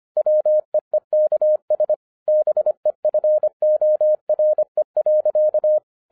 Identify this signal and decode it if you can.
Morse code, heard as a keyed tone, about 25 wpm, 610 Hz. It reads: WEEKS BEFORE.